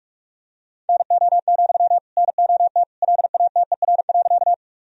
Morse code, heard as a keyed tone, about 46 wpm, 700 Hz. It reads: NO, NOT LATER.
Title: Sentences from Top 1000 Words - Review Part 4 - 45wpm